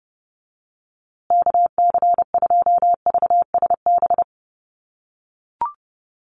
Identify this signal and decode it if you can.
Morse code, heard as a keyed tone, about 30 wpm, 700 Hz. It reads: KC2VSB
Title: USA Call Signs Repeated 3x 30wpm